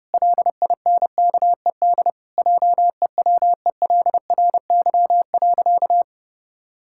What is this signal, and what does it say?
Morse code, heard as a keyed tone, about 32 wpm, 700 Hz. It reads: LINKED JEWELRY.
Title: Pangram Challenge v2 - 30wpm